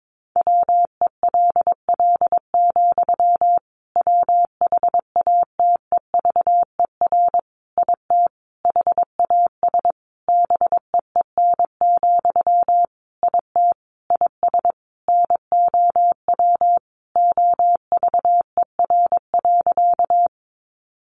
Morse code, heard as a keyed tone, about 22 wpm, 700 Hz. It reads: WELL, WHATEVER IT HAS BEEN, IT IS NOW OVER.
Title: Sentences from Top 1100 Words 22wpm